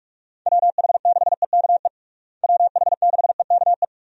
Morse code, heard as a keyed tone, about 46 wpm, 700 Hz. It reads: WH6EKE WH6EKE
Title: USA Call Signs - 6 Characters - Repeated 2x - 45wpm